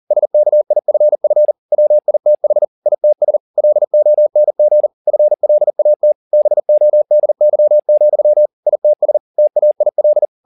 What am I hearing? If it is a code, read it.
Morse code, heard as a keyed tone, about 40 wpm, 600 Hz. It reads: SKIFF WITH ITS LONG FLAT BODY, ITS TAIL
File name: the old man and the sea 2 40wpm